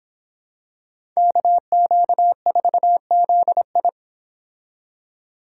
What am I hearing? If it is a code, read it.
Morse code, heard as a keyed tone, about 26 wpm, 700 Hz. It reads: KQ4ZI